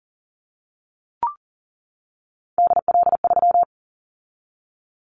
Morse code, heard as a keyed tone, about 39 wpm, 700 Hz. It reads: DL3
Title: Callsign Prefixes for Top 50 DXCC - 40wpm